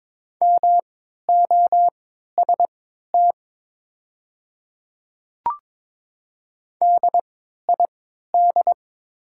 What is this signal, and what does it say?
Morse code, heard as a keyed tone, about 22 wpm, spaced as for 13 wpm, 700 Hz. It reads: MOST DID
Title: 3x Character Spacing - Top 200 Words - 22wpm